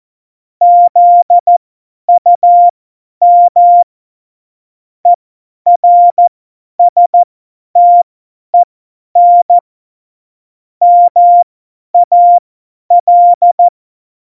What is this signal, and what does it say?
Morse code, heard as a keyed tone, about 14 wpm, spaced as for 10 wpm, 700 Hz. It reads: ZUM ERSTEN MAL
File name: Morsekurs / Witze / Witz 11. Februar 2020, 50 F70